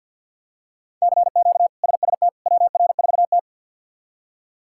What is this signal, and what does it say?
Morse code, heard as a keyed tone, about 52 wpm, 700 Hz. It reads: CQ SST WR4T
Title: K1USN SST QSO Practice - 50wpm